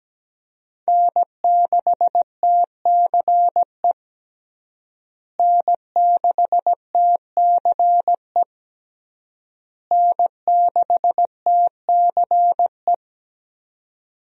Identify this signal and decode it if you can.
Morse code, heard as a keyed tone, about 17 wpm, 700 Hz. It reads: N6TCE N6TCE N6TCE
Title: USA Call Signs - 5 Characters - Repeated 3x - 17wpm